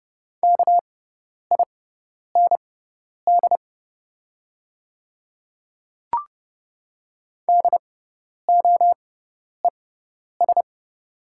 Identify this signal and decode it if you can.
Morse code, heard as a keyed tone, about 30 wpm, spaced as for 10 wpm, 690 Hz. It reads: KIND DOES
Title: 6x Character Spacing - Top 200 Words - 30wpm